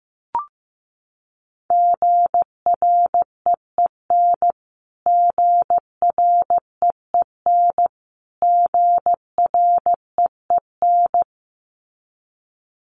Morse code, heard as a keyed tone, about 15 wpm, 700 Hz. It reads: GREEN GREEN GREEN
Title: Top 500 Words Repeated 15wpm